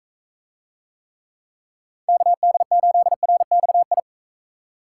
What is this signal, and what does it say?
Morse code, heard as a keyed tone, about 40 wpm, 700 Hz. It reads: KD8RXI